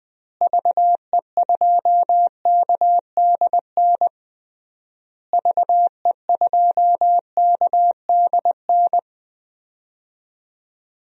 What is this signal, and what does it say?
Morse code, heard as a keyed tone, about 21 wpm, 700 Hz. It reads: VE2KDN VE2KDN
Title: DXCC-100 Call Signs - 6 Characters - Repeated 3x - 20wpm